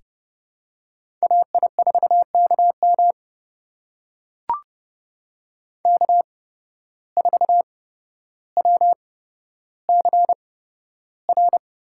Morse code, heard as a keyed tone, about 30 wpm, spaced as for 13 wpm, 700 Hz. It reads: AI4KM K4WCR
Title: USA Call Signs - 5 Characters - 8x Character Spacing - 30wpm